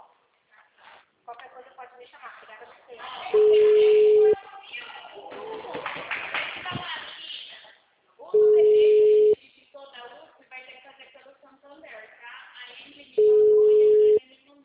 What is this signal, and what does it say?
Ringback tone (Europe, 425 Hz), with speech.
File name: out- -2029- - - .38796